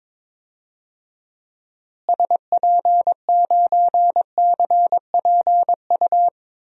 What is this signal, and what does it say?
Morse code, heard as a keyed tone, about 22 wpm, 700 Hz. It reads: SP9CPU
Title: DXCC-100 Call Signs - 6 Characters - Repeated 3x - 22wpm